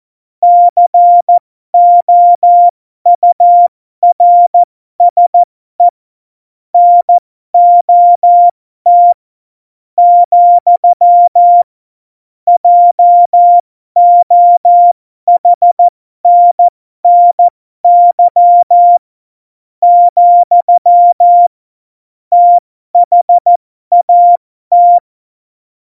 Morse code, heard as a keyed tone, about 14 wpm, spaced as for 12 wpm, 700 Hz. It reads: COURSE NOT , JOHNNY , THAT
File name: Morsekurs / Witze